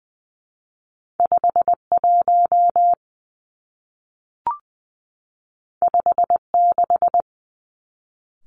Morse code, heard as a keyed tone, about 20 wpm, 700 Hz. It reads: 5 1 5 6
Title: Two Challenging Letter-Number Combinations 20wpm